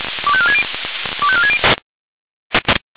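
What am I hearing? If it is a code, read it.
Five-tone selective call (ZVEI), heard as quick runs of five tones, tones of about 70 ms. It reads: 25460 25460
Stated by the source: ZVEI five-tone selective call 25459 received by SDR.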